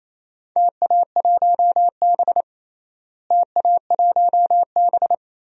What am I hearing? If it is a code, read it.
Morse code, heard as a keyed tone, about 27 wpm, 700 Hz. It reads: TA1B TA1B
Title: DXCC-100 Call Signs - 4 Characters - Repeated 3x - 28wpm